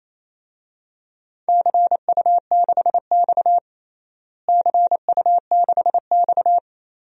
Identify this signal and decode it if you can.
Morse code, heard as a keyed tone, about 28 wpm, 700 Hz. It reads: CU6X CU6X
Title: DXCC-100 Call Signs - 4 Characters - Repeated 3x - 28wpm